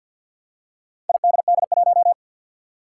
Morse code, heard as a keyed tone, about 47 wpm, 700 Hz. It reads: IDD1
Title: Four Challenging Letter-Number Combinations 50wpm